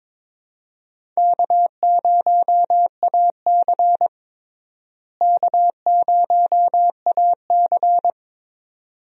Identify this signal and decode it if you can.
Morse code, heard as a keyed tone, about 22 wpm, 700 Hz. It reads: K0AC K0AC
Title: USA Call Signs - 4 Characters - Repeated 2x - 22wpm